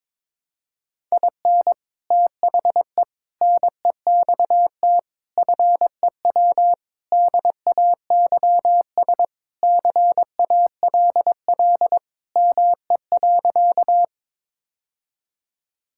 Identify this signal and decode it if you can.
Morse code, heard as a keyed tone, about 22 wpm, 700 Hz. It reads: IN THE NEXT FEW DAYS CALL ME.
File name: practice / 22wpm / Sentences with 3-Word Phrases - 22wpm